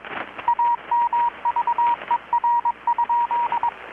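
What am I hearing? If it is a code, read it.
Morse code, heard as a keyed tone, about 21 wpm, 950 Hz, with radio noise. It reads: AMVER?
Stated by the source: Morse from KPH's Night of Nights XX maritime radiotelegraphy event, received in Chiapas, Mexico about 3,800 km away on a Kenwood R-600 with a grounded 10 m long-wire antenna.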